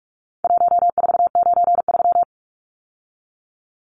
Morse code, heard as a keyed tone, about 44 wpm, 700 Hz. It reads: 1493
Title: Numbers - Contest Style - 4-Digit 45wpm